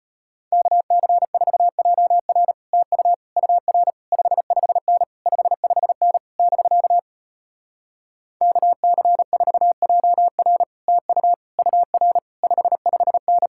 Morse code, heard as a keyed tone, about 33 wpm, 700 Hz. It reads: KC4JR TU UR 55N 55N <BK> KC4JR TU UR 55N